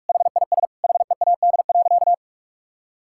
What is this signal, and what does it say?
Morse code, heard as a keyed tone, about 41 wpm, 700 Hz. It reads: HIS HEAD.
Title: Sentences from Top 1000 Words - Review Part 6 - 45wpm